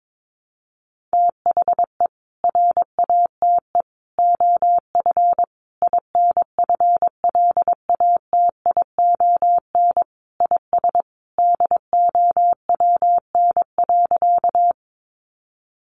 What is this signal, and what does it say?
Morse code, heard as a keyed tone, about 22 wpm, 700 Hz. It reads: THE RATE OF INFLATION IS DOWN.
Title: Sentences from Top 1100 Words 22wpm